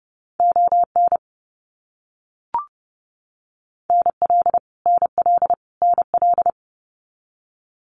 Morse code, heard as a keyed tone, about 30 wpm, 700 Hz. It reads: ON NL NL NL
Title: Canadian Provinces and Territories - Repeated 3x - 30wpm